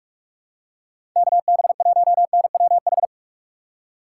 Morse code, heard as a keyed tone, about 41 wpm, 700 Hz. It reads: KB1NWH
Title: USA Call Signs - 2x Character Spacing - 45wpm